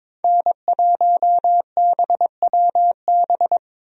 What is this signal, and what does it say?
Morse code, heard as a keyed tone, about 22 wpm, 700 Hz. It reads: N1BWB
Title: USA Call Signs - 5 Characters - 22wpm